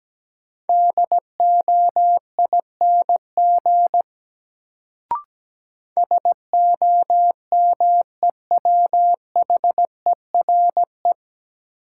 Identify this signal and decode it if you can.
Morse code, heard as a keyed tone, about 17 wpm, 700 Hz. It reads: DOING SOMEWHERE
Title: Sentences from Top 1400 Words - New Word Drill - 17wpm